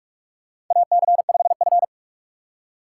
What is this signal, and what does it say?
Morse code, heard as a keyed tone, about 43 wpm, 700 Hz. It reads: AC5F
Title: USA Call Signs - 4 Characters - 6x Character Spacing - 45wpm